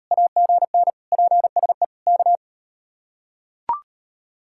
Morse code, heard as a keyed tone, about 37 wpm, 700 Hz. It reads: AGN PSE K